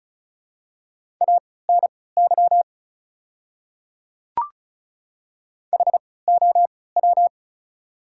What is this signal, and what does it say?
Morse code, heard as a keyed tone, about 34 wpm, spaced as for 20 wpm, 700 Hz. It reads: ANY HOW
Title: 3x Character Spacing - Top 200 Words - 35wpm